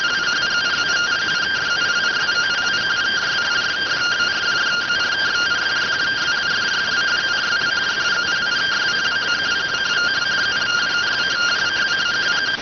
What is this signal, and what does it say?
Two-tone teleprinter signal, about 170 Hz shift.